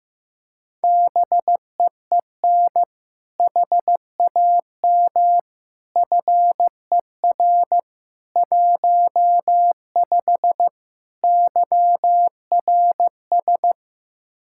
Morse code, heard as a keyed tone, about 16 wpm, 700 Hz. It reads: BEEN HAM FER 15 YRS
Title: K1USN SST QSO Practice 15wpm